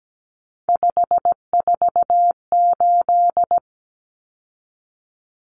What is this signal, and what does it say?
Morse code, heard as a keyed tone, about 17 wpm, 700 Hz. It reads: 548
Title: Numbers - 3-Digit 17wpm